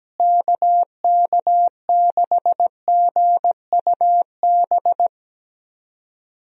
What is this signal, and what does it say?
Morse code, heard as a keyed tone, about 17 wpm, 700 Hz. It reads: KK6GUB
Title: USA Call Signs - 6 Characters - Repeated 3x - 17wpm